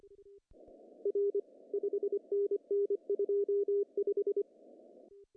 Morse code, heard as a keyed tone, about 24 wpm, 410 Hz. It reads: R 5NN 25
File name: CW / cqww-cw-k5tr